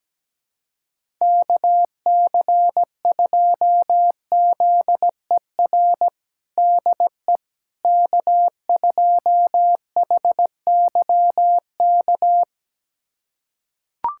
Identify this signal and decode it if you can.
Morse code, heard as a keyed tone, about 17 wpm, 700 Hz. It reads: KC2ZER DE K2HYK
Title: USA Callsign DE Callsign 17wpm